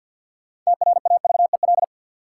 Morse code, heard as a keyed tone, about 53 wpm, 700 Hz. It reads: TRAVEL
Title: Sentences from Top 400 Words - New Word Drill - 50wpm